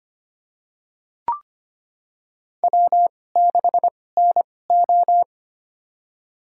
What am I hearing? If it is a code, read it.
Morse code, heard as a keyed tone, about 25 wpm, spaced as for 18 wpm, 700 Hz. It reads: W6NO
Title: USA Call Signs - 4 Characters - 2x Character Spacing - 25wpm